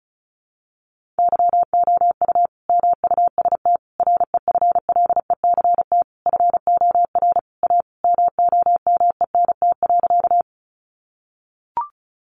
Morse code, heard as a keyed tone, about 34 wpm, 700 Hz. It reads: YOU MUST REFLECT FOR A MOMENT.
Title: Sentences from Top 1500 Words 35wpm